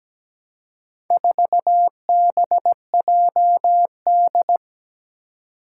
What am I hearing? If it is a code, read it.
Morse code, heard as a keyed tone, about 17 wpm, 700 Hz. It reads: 4BJD
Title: Four Challenging Letter-Number Combinations 17wpm